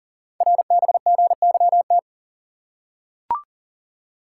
Morse code, heard as a keyed tone, about 38 wpm, 700 Hz. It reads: RBGYT
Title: Code Groups - 5 Letters - 40wpm